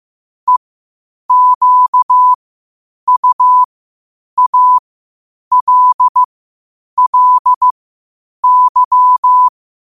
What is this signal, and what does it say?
Morse code, heard as a keyed tone, about 15 wpm, spaced as for 8 wpm, 1000 Hz. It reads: EQUALLY
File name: qotdfiles / 8wpm